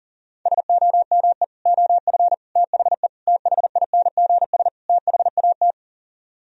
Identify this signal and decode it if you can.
Morse code, heard as a keyed tone, about 53 wpm, 700 Hz. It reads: SOME OF THE THINGS THAT